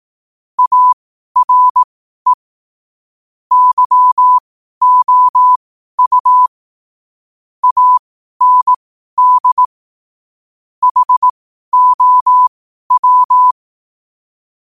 Morse code, heard as a keyed tone, about 18 wpm, spaced as for 12 wpm, 1000 Hz. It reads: ARE YOU AND HOW
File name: qotdfiles / 12wpm